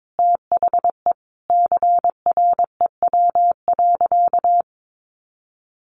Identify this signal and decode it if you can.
Morse code, heard as a keyed tone, about 22 wpm, 700 Hz. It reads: THE CREW.
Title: Sentences from Top 1500 Words 22wpm